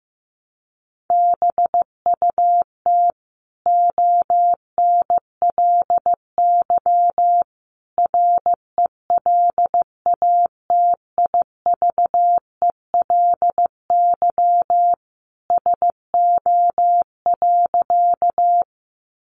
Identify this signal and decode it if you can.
Morse code, heard as a keyed tone, about 15 wpm, 700 Hz. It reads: BUT ONLY RELATIVELY SO.
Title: Sentences from Top 1100 Words 15wpm